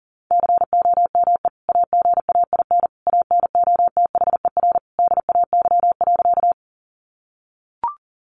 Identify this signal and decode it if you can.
Morse code, heard as a keyed tone, about 40 wpm, 700 Hz. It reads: COME AGAIN ANOTHER DAY.
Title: Sentences from Top 600 Words 40wpm